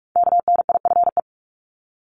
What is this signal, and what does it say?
Morse code, heard as a keyed tone, about 46 wpm, 700 Hz. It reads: KNIFE
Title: Multi-letter Phonemes 45wpm